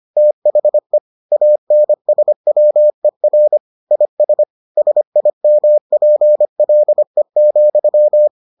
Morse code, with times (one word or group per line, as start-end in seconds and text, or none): 0.2-1.0 THE
1.3-3.6 ANSWER
3.9-4.4 IS
4.8-8.3 SIMPLE,